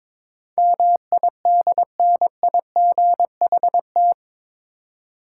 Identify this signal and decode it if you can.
Morse code, heard as a keyed tone, about 22 wpm, 700 Hz. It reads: MIDNIGHT
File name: practice / 22wpm / Eight Letter Words - 22wpm